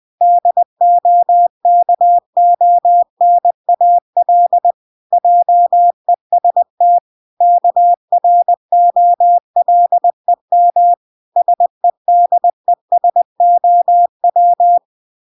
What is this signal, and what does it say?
Morse code, heard as a keyed tone, about 20 wpm, 700 Hz. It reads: DOKONAL JEST KROLEM SEDESOW